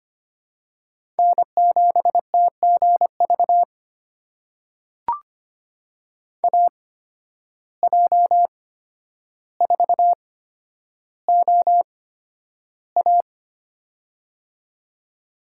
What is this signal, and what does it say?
Morse code, heard as a keyed tone, about 25 wpm, spaced as for 11 wpm, 700 Hz. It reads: N7TGV AJ4OA
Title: USA Call Signs - 5 Characters - 8x Character Spacing - 25wpm